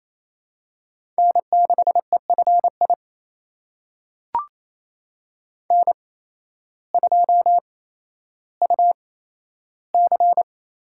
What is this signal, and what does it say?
Morse code, heard as a keyed tone, about 28 wpm, spaced as for 13 wpm, 700 Hz. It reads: N6EFI N2UC
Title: USA Call Signs - 5 Characters - 8x Character Spacing - 28wpm